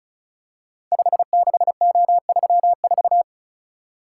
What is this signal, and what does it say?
Morse code, heard as a keyed tone, about 36 wpm, 700 Hz. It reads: 56O34
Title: Code Groups - 5 Letter:Numbers - 35wpm